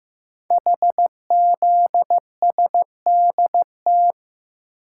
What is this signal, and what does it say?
Morse code, heard as a keyed tone, about 15 wpm, 700 Hz. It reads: HZSDT
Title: Code Groups - 5 Letters - 15wpm